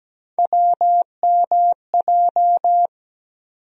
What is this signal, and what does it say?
Morse code, heard as a keyed tone, about 17 wpm, 700 Hz. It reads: WMJ
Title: USA Call Sign Postfixes - 3 Letters 17wpm